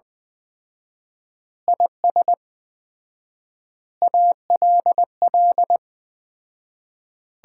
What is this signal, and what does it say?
Morse code, heard as a keyed tone, about 20 wpm, 700 Hz. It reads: IS ALL
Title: 4x Word-Spacing - Sets of 5 Words - 20wpm